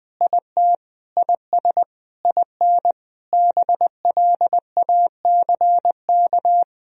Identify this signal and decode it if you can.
Morse code, heard as a keyed tone, about 20 wpm, 700 Hz. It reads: IT IS IN BLACK
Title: Sentences from Top 1000 Words - Review Part 2 - 20wpm